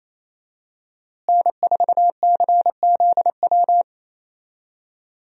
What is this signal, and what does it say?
Morse code, heard as a keyed tone, about 28 wpm, 700 Hz. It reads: N4CZW
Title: USA Call Signs - 5 Characters - Repeated 2x - 28wpm